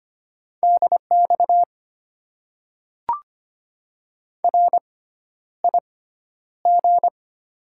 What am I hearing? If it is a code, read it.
Morse code, heard as a keyed tone, about 25 wpm, spaced as for 11 wpm, 700 Hz. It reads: DX RIG
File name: Top 90 QSO Elements - 6x Character Spacing - 25wpm